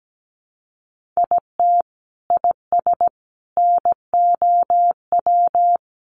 Morse code, heard as a keyed tone, about 17 wpm, 700 Hz. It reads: IT IS NOW
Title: Sentences from Top 1400 Words 17wpm